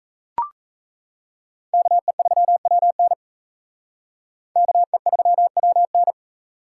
Morse code, heard as a keyed tone, about 35 wpm, 700 Hz. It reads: KE3WN KE3WN